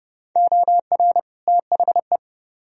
Morse code, heard as a keyed tone, about 40 wpm, 700 Hz. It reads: OR THE